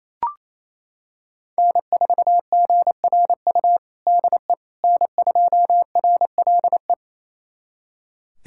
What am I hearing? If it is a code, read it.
Morse code, heard as a keyed tone, about 28 wpm, 700 Hz. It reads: N4GRU DE N2RLE
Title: USA Callsign DE Callsign - 5 Characters - 28wpm